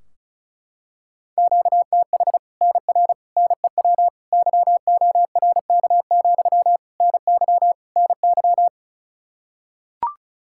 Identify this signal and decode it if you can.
Morse code, heard as a keyed tone, about 34 wpm, 700 Hz. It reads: QTH NR NEW YORK, NY NY
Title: Practice QSO - Exchange 1 - Beginning - 35wpm